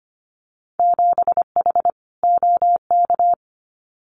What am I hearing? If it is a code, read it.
Morse code, heard as a keyed tone, about 25 wpm, 700 Hz. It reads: 7H OK
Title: Winter Field Day Exchanges Encore 25wpm